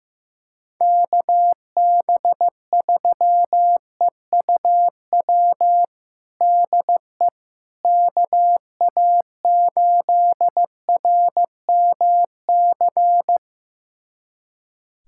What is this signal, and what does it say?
Morse code, heard as a keyed tone, about 15 wpm, 700 Hz. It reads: KB3EUW DE KA8RMC
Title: USA Callsign DE Callsign - 6 Characters - 15wpm